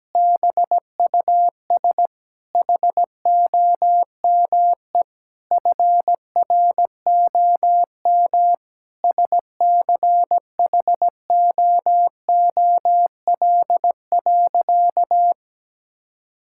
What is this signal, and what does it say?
Morse code, heard as a keyed tone, about 17 wpm, 700 Hz. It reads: BUS HOME FROM SCHOOL.
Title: Sentences from Top 1000 Words - Review Part 6 - 17wpm